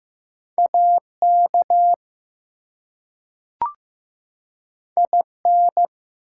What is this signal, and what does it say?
Morse code, heard as a keyed tone, about 15 wpm, 700 Hz. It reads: AK IN
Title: US States and Canadian Provinces 15wpm